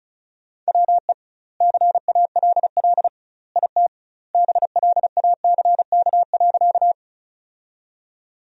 Morse code, heard as a keyed tone, about 34 wpm, 700 Hz. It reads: WE CALL IT BLACK.
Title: 2x Word-Spacing - Sentences from Top 400 Words - 35wpm